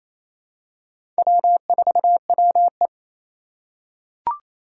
Morse code, heard as a keyed tone, about 28 wpm, 700 Hz. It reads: W4WE
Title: Four Challenging Letter-Number Combinations 28wpm